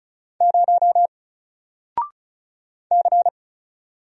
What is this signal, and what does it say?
Morse code, heard as a keyed tone, about 35 wpm, 700 Hz. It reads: 0 C
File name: Single-Letter-Number-35wpm